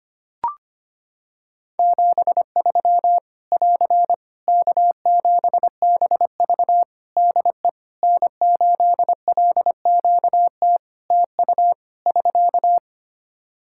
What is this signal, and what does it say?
Morse code, heard as a keyed tone, about 25 wpm, 700 Hz. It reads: 73 <AR> K7BV DE N8LQT TU <SK>
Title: Practice QSO - Exchange 4 - Ending - 25wpm